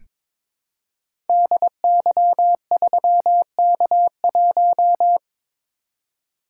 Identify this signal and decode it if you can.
Morse code, heard as a keyed tone, about 22 wpm, 700 Hz. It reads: DY3K1
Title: Code Groups - 5 Letter:Numbers - 22wpm